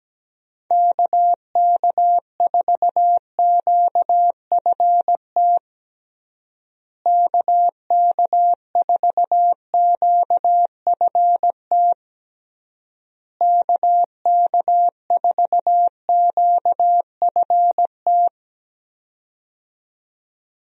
Morse code, heard as a keyed tone, about 17 wpm, 700 Hz. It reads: KK4QFT KK4QFT KK4QFT
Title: USA Call Signs - 6 Characters - Repeated 3x - 17wpm